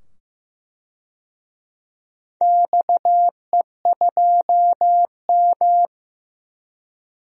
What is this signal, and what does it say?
Morse code, heard as a keyed tone, about 15 wpm, 700 Hz. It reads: XE2M
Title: DXCC-100 Call Signs - 4 Characters - 15wpm